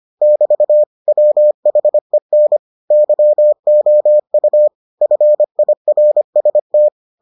Morse code, heard as a keyed tone, about 25 wpm, 600 Hz. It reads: <BT> WHEN YOU FIRST